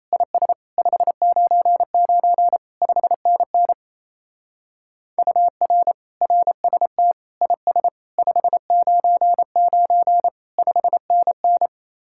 Morse code, heard as a keyed tone, about 35 wpm, 700 Hz. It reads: IS 599 5NN UR RST IS 599 5NN
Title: Speed-Racing Common QSO Phrases Encore - 28wpm